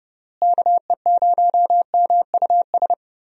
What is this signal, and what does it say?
Morse code, heard as a keyed tone, about 29 wpm, 700 Hz. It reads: KE0MUS
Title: USA Call Signs - 6 Characters - Repeated 3x - 30wpm